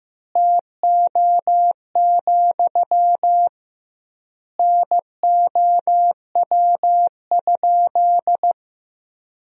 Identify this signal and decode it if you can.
Morse code, heard as a keyed tone, about 14 wpm, 700 Hz. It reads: TO, NOW?
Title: 2x Word-Spacing - Sentences from Top 200 Words - 15wpm